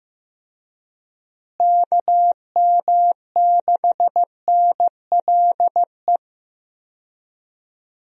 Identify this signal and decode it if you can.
Morse code, heard as a keyed tone, about 15 wpm, 700 Hz. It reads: KM6NLE
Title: USA Call Signs - 6 Characters - Repeated 3x - 15wpm